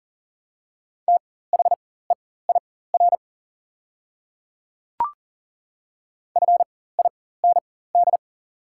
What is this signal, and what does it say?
Morse code, heard as a keyed tone, about 40 wpm, spaced as for 19 wpm, 700 Hz. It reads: THEIR FIND
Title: 4x Character Spacing - Top 100 Words - 40wpm